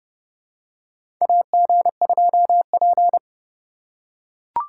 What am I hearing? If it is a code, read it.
Morse code, heard as a keyed tone, about 29 wpm, 700 Hz. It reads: AG2P
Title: USA Call Signs - 4 Characters - 3x Character Spacing - 30wpm